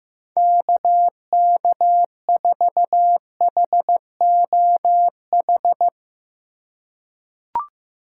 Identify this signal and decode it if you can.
Morse code, heard as a keyed tone, about 15 wpm, 700 Hz. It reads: KK4HOH
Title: USA Call Signs - 6 Characters - 3x Character Spacing - 15wpm